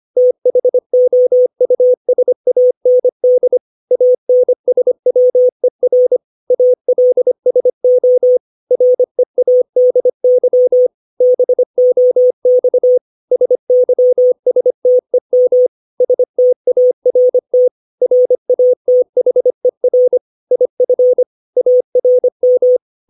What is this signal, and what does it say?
Morse code, heard as a keyed tone, about 25 wpm, 500 Hz. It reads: THOUSAND ANSWER ALSO READY BOX SYSTEM START RATHER IF ARM
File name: 632 words / 632 Words 25WPM